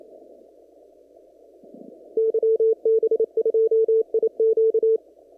Morse code, heard as a keyed tone, about 27 wpm, 450 Hz. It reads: YB2IQ
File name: CW / kt5j wpx cw-op-k5pi